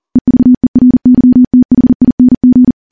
Morse code, heard as a keyed tone, about 40 wpm, 250 Hz. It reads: EVERYTHING
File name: morse / morse-training-40WPM